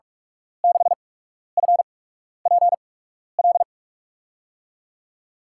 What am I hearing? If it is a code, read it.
Morse code, heard as a keyed tone, about 44 wpm, spaced as for 12 wpm, 700 Hz. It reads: 6FPL
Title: USA Call Signs - 6 Characters - 8x Character Spacing - 45wpm